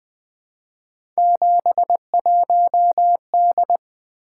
Morse code, heard as a keyed tone, about 20 wpm, 700 Hz. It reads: 71D
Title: Three Letter-Number Combinations - 20wpm